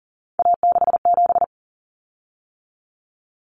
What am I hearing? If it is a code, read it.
Morse code, heard as a keyed tone, about 40 wpm, 700 Hz. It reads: A67
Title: Common Cut Numbers - 3-Digit 40wpm